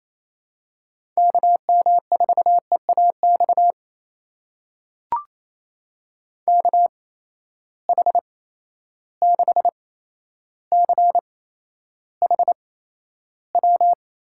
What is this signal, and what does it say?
Morse code, heard as a keyed tone, about 28 wpm, spaced as for 12 wpm, 700 Hz. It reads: KM4EAX KH6CHW
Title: USA Call Signs - 6 Characters - 8x Character Spacing - 28wpm